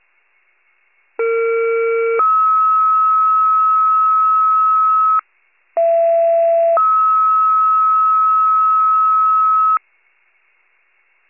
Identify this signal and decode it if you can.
Two-tone page, heard as a one-second tone, then a three-second tone, about 460 then 1290 Hz, about 670 then 1290 Hz. It is heard from 1.2 to 9.8 s.